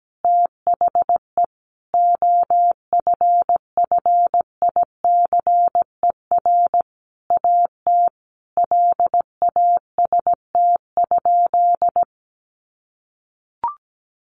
Morse code, heard as a keyed tone, about 17 wpm, 700 Hz. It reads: THE OFFICER AT LAST?
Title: Sentences from Top 1300 Words 17wpm